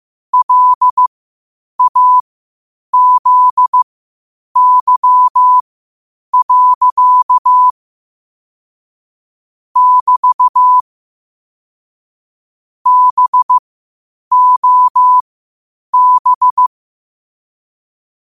Morse code, heard as a keyed tone, about 15 wpm, spaced as for 8 wpm, 1000 Hz. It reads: LAZY. <BT> BOB